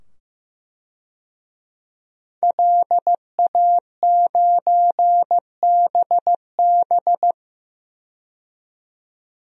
Morse code, heard as a keyed tone, about 15 wpm, 700 Hz. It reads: LA9BB
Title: DXCC-100 Call Signs - 5 Characters - 15wpm